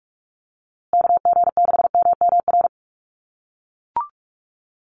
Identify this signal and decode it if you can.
Morse code, heard as a keyed tone, about 44 wpm, 700 Hz. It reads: KG6MMR
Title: USA Call Signs Repeated 3x 45wpm